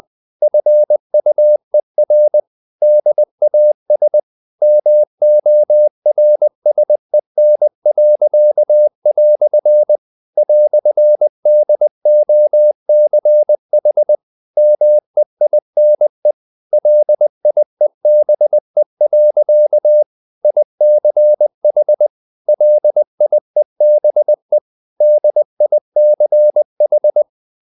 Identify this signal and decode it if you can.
Morse code, heard as a keyed tone, about 20 wpm, 600 Hz. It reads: FUER DAS MORSEN." "DOCH MEINE LIEBE. ICH LIEBE DICH
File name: dk5ke / Humor-100BpM